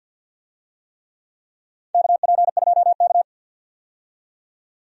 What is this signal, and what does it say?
Morse code, heard as a keyed tone, about 48 wpm, 700 Hz. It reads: KP2K
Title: DXCC-100 Call Signs - 4 Characters - 50wpm